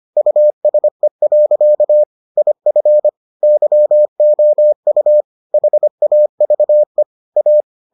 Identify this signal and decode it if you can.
Morse code, heard as a keyed tone, about 25 wpm, 600 Hz. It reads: USE. IF YOU HAVE A